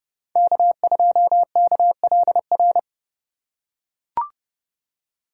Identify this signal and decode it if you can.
Morse code, heard as a keyed tone, about 30 wpm, 700 Hz. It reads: K2KLR
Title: USA Call Signs - 4x Character Spacing - 30wpm